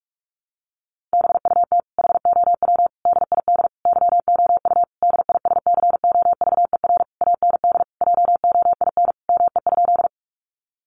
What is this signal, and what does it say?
Morse code, heard as a keyed tone, about 44 wpm, 700 Hz. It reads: BUT HOW DID YOU DISCOVER AND JOIN ME?
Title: Sentences from Top 1400 Words 45wpm